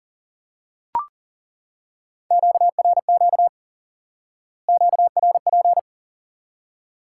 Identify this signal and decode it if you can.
Morse code, heard as a keyed tone, about 40 wpm, 700 Hz. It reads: QRQ QRP
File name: IQE Course 3 - Lesson 89 - YAGI - Two Elements - 40wpm